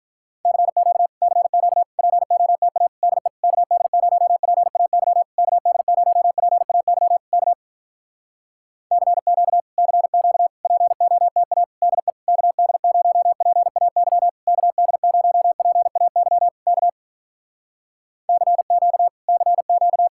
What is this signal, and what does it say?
Morse code, heard as a keyed tone, about 46 wpm, 700 Hz. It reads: CQ CQ POTA DE KD0PAY KD0PAY K CQ CQ POTA DE KD0PAY KD0PAY K CQ CQ